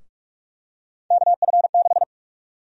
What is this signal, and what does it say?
Morse code, heard as a keyed tone, about 46 wpm, 700 Hz. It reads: KF6
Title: USA Call Sign Prefixes - 45wpm